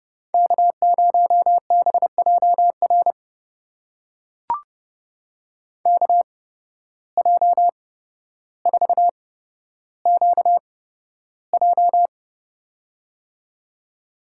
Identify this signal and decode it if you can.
Morse code, heard as a keyed tone, about 30 wpm, spaced as for 13 wpm, 700 Hz. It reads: K0BJR KJ4QJ
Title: USA Call Signs - 5 Characters - 8x Character Spacing - 30wpm